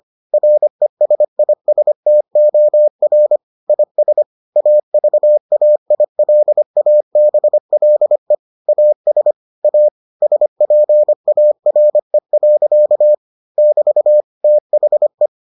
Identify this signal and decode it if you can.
Morse code, heard as a keyed tone, about 25 wpm, 600 Hz. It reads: RESISTOR IS AVAILABLE AS A SPARE. <BT> THE